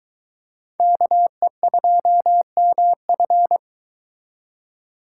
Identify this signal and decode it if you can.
Morse code, heard as a keyed tone, about 23 wpm, 700 Hz. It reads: KE2MF